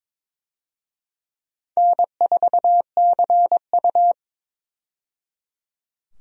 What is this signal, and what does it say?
Morse code, heard as a keyed tone, about 22 wpm, 700 Hz. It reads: N4CU
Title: USA Call Signs - 4 Characters - 22wpm